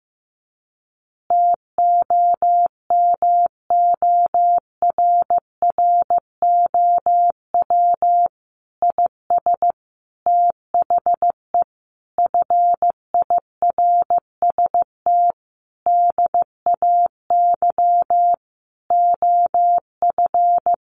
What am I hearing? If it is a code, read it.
Morse code, heard as a keyed tone, about 15 wpm, 700 Hz. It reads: TOMORROW IS THE FIRST DAY OF
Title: Additional Easy Phrases 15wpm